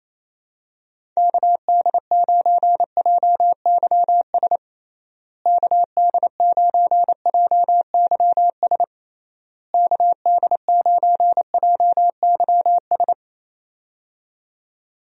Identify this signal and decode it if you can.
Morse code, heard as a keyed tone, about 28 wpm, 700 Hz. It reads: KD9JYS KD9JYS KD9JYS
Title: USA Call Signs - 6 Characters - Repeated 3x - 28wpm